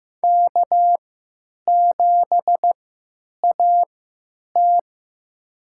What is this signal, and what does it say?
Morse code, heard as a keyed tone, about 15 wpm, spaced as for 9 wpm, 700 Hz. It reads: K7AT
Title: USA Call Signs - 4 Characters - 3x Character Spacing - 15wpm